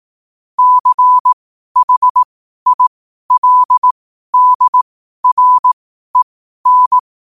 Morse code, heard as a keyed tone, about 18 wpm, spaced as for 12 wpm, 1000 Hz. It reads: CHILDREN
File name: qotdfiles / 12wpm